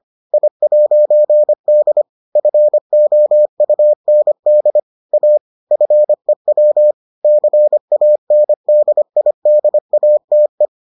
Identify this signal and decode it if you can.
Morse code, heard as a keyed tone, about 25 wpm, 600 Hz. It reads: I'D FOUND A FEW CANDIDATE